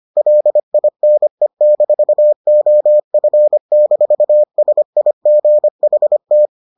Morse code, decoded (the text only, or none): LINE-OF-SIGHT